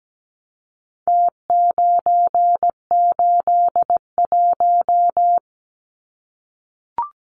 Morse code, heard as a keyed tone, about 17 wpm, 700 Hz. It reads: T981
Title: Numbers - Contest Style - 4-Digit 17wpm